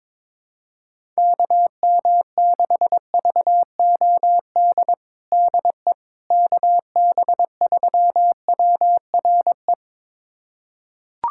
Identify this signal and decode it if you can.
Morse code, heard as a keyed tone, about 22 wpm, 700 Hz. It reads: KM6VOD DE KB3WRE
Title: USA Callsign DE Callsign 22wpm